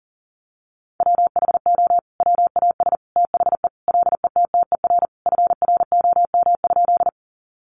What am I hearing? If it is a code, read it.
Morse code, heard as a keyed tone, about 40 wpm, 700 Hz. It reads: WHO WAS THE LETTER FROM?
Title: Sentences from Top 300 Words 40wpm